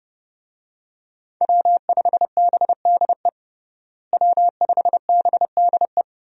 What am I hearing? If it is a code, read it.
Morse code, heard as a keyed tone, about 30 wpm, 700 Hz. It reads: W5BDE W5BDE
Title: USA Call Signs - 5 Characters - Repeated 2x - 30wpm